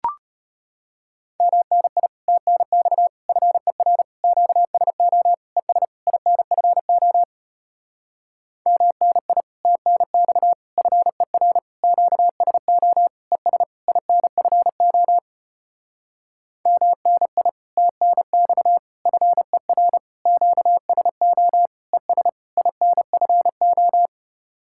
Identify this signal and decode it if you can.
Morse code, heard as a keyed tone, about 33 wpm, 700 Hz. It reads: MNI TNX FER QSO ES INFO MNI TNX FER QSO ES INFO MNI TNX FER QSO ES INFO